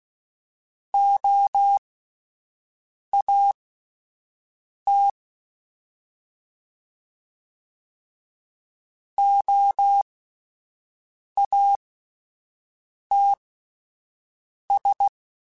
Morse code, heard as a keyed tone, about 16 wpm, spaced as for 5 wpm, 800 Hz. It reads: OAT OATS
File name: Alphabetical / Alphabetical mp3s / Os @5